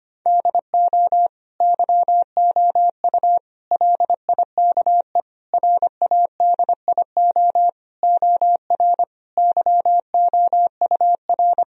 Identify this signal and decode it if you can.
Morse code, heard as a keyed tone, about 25 wpm, 700 Hz. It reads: DO YOU LIKE RADIO OR YOUR
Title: Sentences from Top 1000 Words - Review Part 4 - 25wpm